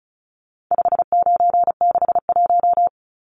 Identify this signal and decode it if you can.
Morse code, heard as a keyed tone, about 34 wpm, 700 Hz. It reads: 5961
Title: Numbers - 4-Digit 35wpm